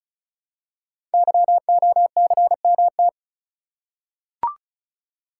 Morse code, heard as a keyed tone, about 36 wpm, 700 Hz. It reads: YOCMT
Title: Code Groups - 5 Letters - 35wpm